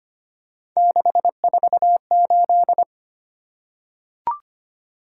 Morse code, heard as a keyed tone, about 25 wpm, 700 Hz. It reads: 648
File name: Three Letter-Number Combinations - 25wpm